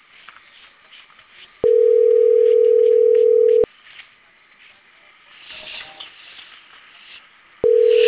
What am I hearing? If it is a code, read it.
Ringback tone (North America), 440 + 480 Hz.
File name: CENTRAL CAEA / out agcaea02 - .16983